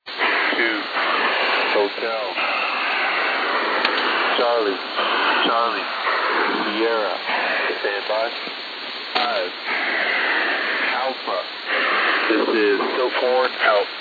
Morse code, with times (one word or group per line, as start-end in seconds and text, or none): none